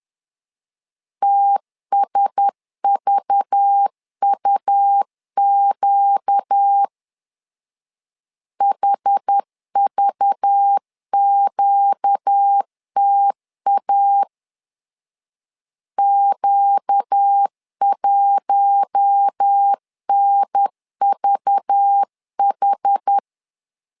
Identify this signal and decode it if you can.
Morse code, heard as a keyed tone, about 10 wpm, 800 Hz. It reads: TSVUQ HVQTA Q1NVH
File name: LEZIONI / 5 LBVC / 5 LBV 10wpm 2248